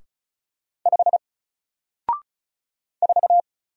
Morse code, heard as a keyed tone, about 36 wpm, 700 Hz. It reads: 5 4
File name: Single-Letter-Number-35wpm